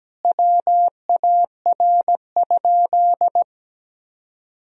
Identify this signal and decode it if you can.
Morse code, heard as a keyed tone, about 17 wpm, 700 Hz. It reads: WAR?
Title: Sentences from Top 1000 Words - Review Part 2 - 17wpm